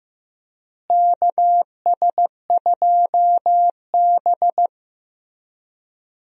Morse code, heard as a keyed tone, about 15 wpm, 700 Hz. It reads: KS2B
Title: USA Call Signs - 4 Characters - Repeated 3x - 15wpm